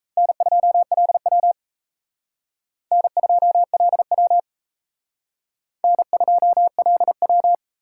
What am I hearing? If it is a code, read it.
Morse code, heard as a keyed tone, about 36 wpm, 700 Hz. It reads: N2LW N2LW N2LW